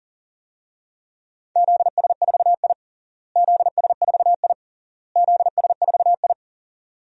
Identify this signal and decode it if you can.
Morse code, heard as a keyed tone, about 40 wpm, 700 Hz. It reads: ZS4I ZS4I ZS4I
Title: DXCC-100 Call Signs - 4 Characters - Repeated 3x - 40wpm